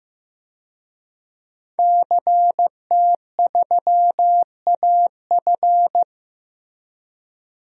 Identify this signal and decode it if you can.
Morse code, heard as a keyed tone, about 15 wpm, 700 Hz. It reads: CT3AF
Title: DXCC-100 Call Signs - 5 Characters - Repeated 2x - 15wpm